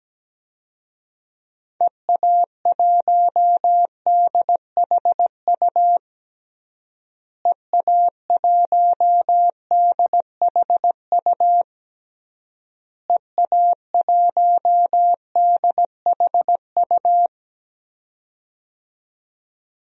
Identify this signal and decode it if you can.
Morse code, heard as a keyed tone, about 17 wpm, 700 Hz. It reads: EA1DHU EA1DHU EA1DHU
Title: DXCC-100 Call Signs - 6 Characters - Repeated 3x - 17wpm